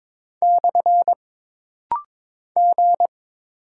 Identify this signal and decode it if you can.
Morse code, heard as a keyed tone, about 22 wpm, 700 Hz. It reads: / G
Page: This rapid-fire timing is geared to push your ICR (Instant Character Recognition) skills.